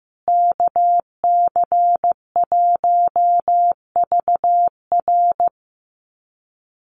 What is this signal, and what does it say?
Morse code, heard as a keyed tone, about 15 wpm, 700 Hz. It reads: KC1VR
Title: USA Call Signs Repeated 3x 15wpm